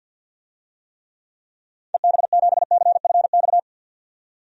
Morse code, heard as a keyed tone, about 47 wpm, 700 Hz. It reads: EB7CFX
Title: DXCC-100 Call Signs - 6 Characters - 50wpm